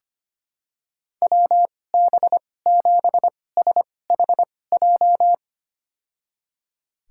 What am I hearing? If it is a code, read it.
Morse code, heard as a keyed tone, about 25 wpm, spaced as for 18 wpm, 700 Hz. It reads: WB7SHJ
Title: USA Call Signs - 2x Character Spacing - 25wpm